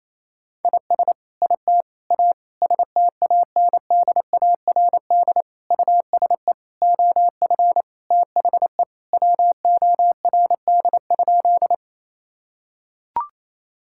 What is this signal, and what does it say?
Morse code, heard as a keyed tone, about 28 wpm, 700 Hz. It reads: IS IT A STANDARD USE OF THE WORD?
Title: Sentences from Top 1000 Words - Review Part 6 - 28wpm